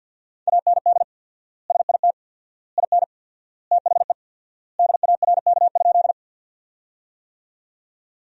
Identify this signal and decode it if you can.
Morse code, heard as a keyed tone, about 50 wpm, 700 Hz. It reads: AND SIT IN THE DARK?
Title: 4x Word-Spacing - Sentences from Top 500 Words - 50wpm